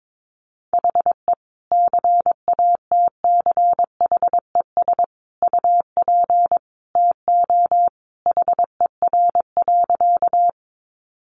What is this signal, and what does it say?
Morse code, heard as a keyed tone, about 22 wpm, 700 Hz. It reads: HE CATCHES UP TO HER.
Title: Additional Easy Phrases 22wpm